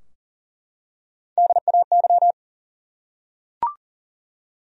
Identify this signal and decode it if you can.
Morse code, heard as a keyed tone, about 40 wpm, 700 Hz. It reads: DAY
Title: Top 100 Words 40wpm